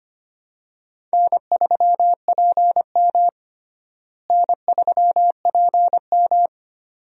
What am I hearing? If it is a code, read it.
Morse code, heard as a keyed tone, about 25 wpm, 700 Hz. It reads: N3PM N3PM
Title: USA Call Signs - 4 Characters - Repeated 2x - 25wpm